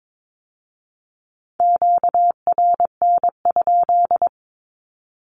Morse code, heard as a keyed tone, about 22 wpm, 700 Hz. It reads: QRN?
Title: Q-Codes with Spoken Meaning - 22wpm